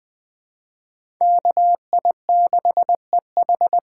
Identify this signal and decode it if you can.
Morse code, heard as a keyed tone, about 20 wpm, 700 Hz. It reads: KI6EH
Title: USA Call Signs - 6 Characters - 6x Character Spacing - 20wpm